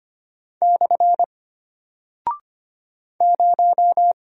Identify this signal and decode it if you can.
Morse code, heard as a keyed tone, about 25 wpm, 700 Hz. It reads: / 0
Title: Single Character (Letter:Number:Punctuation:Prosign) Alt 25wpm